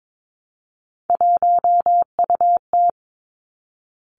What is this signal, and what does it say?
Morse code, heard as a keyed tone, about 22 wpm, 700 Hz. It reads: 1UT
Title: Three Challenging Letter-Number Combinations 22wpm